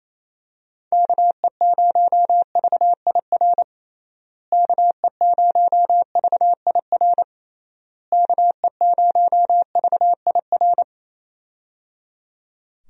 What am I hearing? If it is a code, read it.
Morse code, heard as a keyed tone, about 28 wpm, 700 Hz. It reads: KE0VIR KE0VIR KE0VIR
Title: USA Call Signs - 6 Characters - Repeated 3x - 28wpm